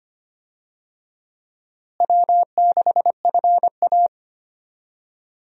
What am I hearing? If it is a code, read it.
Morse code, heard as a keyed tone, about 25 wpm, 700 Hz. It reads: W6FA
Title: USA Call Signs - 4 Characters - Repeated 3x - 25wpm